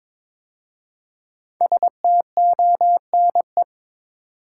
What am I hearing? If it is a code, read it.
Morse code, heard as a keyed tone, about 23 wpm, 700 Hz. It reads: STONE
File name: 5 Letter Words - First 3 Letters Given - 22wpm